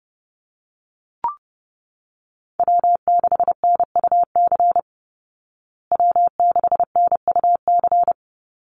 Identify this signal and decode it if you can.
Morse code, heard as a keyed tone, about 30 wpm, 700 Hz. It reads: W6NUC W6NUC
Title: Call Signs Repeated 2x - 30wpm